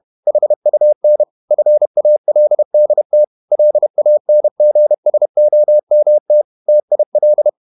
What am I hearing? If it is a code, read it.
Morse code, heard as a keyed tone, about 31 wpm, 600 Hz. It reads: HUN FALDT LANGSOMT TIL
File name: Ferie QRQ 31wpm